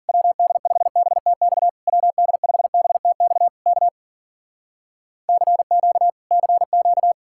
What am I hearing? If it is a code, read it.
Morse code, heard as a keyed tone, about 58 wpm, 700 Hz. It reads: WD5BTX WD5BTX K CQ CQ